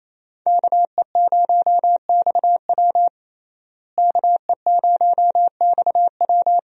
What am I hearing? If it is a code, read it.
Morse code, heard as a keyed tone, about 27 wpm, 700 Hz. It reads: KE0XW KE0XW
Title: USA Call Signs - 5 Characters - Repeated 2x - 28wpm